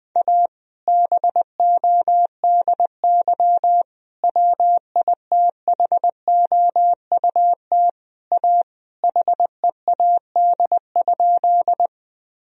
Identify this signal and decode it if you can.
Morse code, heard as a keyed tone, about 20 wpm, 700 Hz. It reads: A BODY WITHOUT A HEAD?
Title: Sentences from Top 1000 Words - Review Part 3 - 20wpm